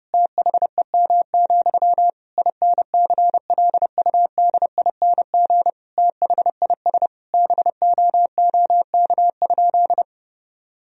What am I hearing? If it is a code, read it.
Morse code, heard as a keyed tone, about 30 wpm, 700 Hz. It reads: THEM, INCLUDING THIS BOOK?
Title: Sentences from Top 1000 Words - Review Part 4 - 30wpm